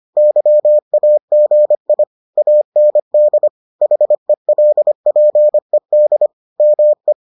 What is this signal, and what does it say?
Morse code, heard as a keyed tone, about 25 wpm, 600 Hz. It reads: YAGI AND HELPED ME